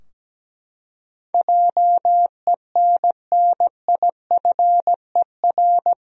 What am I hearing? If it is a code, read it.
Morse code, heard as a keyed tone, about 17 wpm, 700 Hz. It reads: JENNIFER
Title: USA Common Names - 17wpm